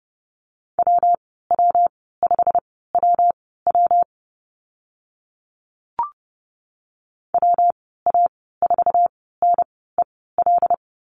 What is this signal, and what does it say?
Morse code, heard as a keyed tone, about 30 wpm, spaced as for 17 wpm, 700 Hz. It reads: WW5WW WA4NEL
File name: Lesson 19g - Letter w - Call Signs 30wpm